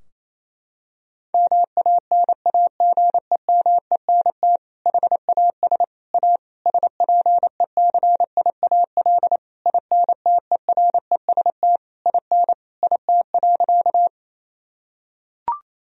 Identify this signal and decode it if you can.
Morse code, heard as a keyed tone, about 28 wpm, 700 Hz. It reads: MANAGEMENT HAS A SPECIAL INTEREST IN IT.
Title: Sentences from Top 1000 Words - Review Part 4 - 28wpm